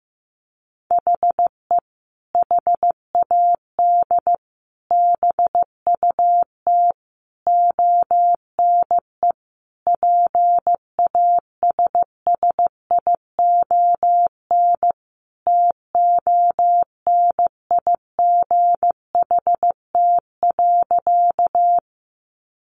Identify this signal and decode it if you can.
Morse code, heard as a keyed tone, about 14 wpm, 700 Hz. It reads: HE HAD BUT ONE PASSION TONIGHT.
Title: Sentences from Top 1300 Words 15wpm